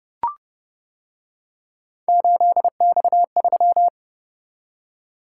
Morse code, heard as a keyed tone, about 29 wpm, 700 Hz. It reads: 8X3